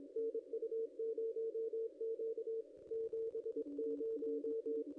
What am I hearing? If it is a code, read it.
Morse code, heard as a keyed tone, about 22 wpm, 440 Hz. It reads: NU0Q 70
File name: ss-cw-k5tr-op-n5rz